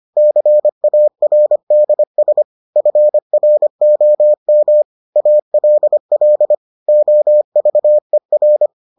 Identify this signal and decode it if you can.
Morse code, heard as a keyed tone, about 25 wpm, 600 Hz. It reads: CARDS FROM ALL OVER